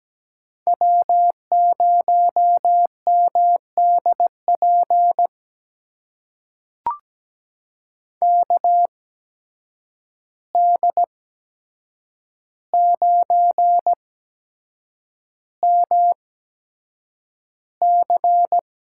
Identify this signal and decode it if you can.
Morse code, heard as a keyed tone, about 17 wpm, spaced as for 7 wpm, 700 Hz. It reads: W0MDP KD9MC
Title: USA Call Signs - 5 Characters - 8x Character Spacing - 17wpm